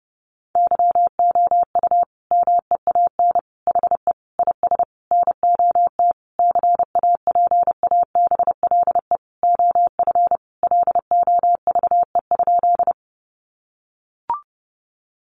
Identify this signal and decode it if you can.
Morse code, heard as a keyed tone, about 30 wpm, 700 Hz. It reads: YOU MEAN HE IS NOT CAPABLE OF LOVE?
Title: Sentences from Top 1200 Words 30wpm